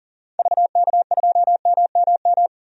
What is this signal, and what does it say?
Morse code, heard as a keyed tone, about 37 wpm, 700 Hz. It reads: VK2MMM
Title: DXCC-100 Call Signs - 6 Characters - 40wpm